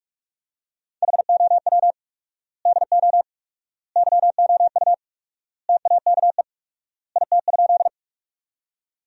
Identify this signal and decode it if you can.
Morse code, heard as a keyed tone, about 44 wpm, 700 Hz. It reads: HOW DO YOU TAKE IT?
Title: 4x Word-Spacing - Sentences from Top 200 Words - 45wpm